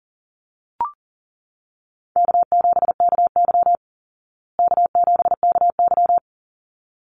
Morse code, heard as a keyed tone, about 40 wpm, 700 Hz. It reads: K7KY K7KY
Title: Easy Call Signs Repeated 2x - 40wpm